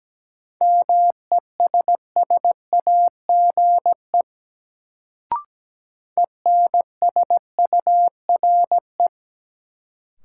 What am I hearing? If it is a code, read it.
Morse code, heard as a keyed tone, about 17 wpm, 700 Hz. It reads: MESSAGE ENSURE
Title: Sentences from Top 1100 Words - New Word Drill - 17wpm